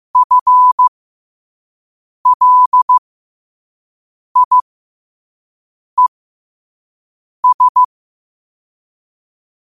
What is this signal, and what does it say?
Morse code, heard as a keyed tone, about 15 wpm, spaced as for 5 wpm, 1000 Hz. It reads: FLIES
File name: qotdfiles / 5wpm